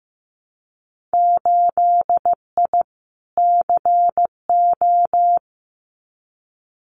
Morse code, heard as a keyed tone, about 15 wpm, 700 Hz. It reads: 8I CO
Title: Winter Field Day Exchanges Encore 15wpm